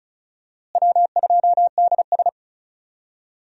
Morse code, heard as a keyed tone, about 34 wpm, 700 Hz. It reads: W2DS
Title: USA Call Signs - 4 Characters - Repeated 3x - 35wpm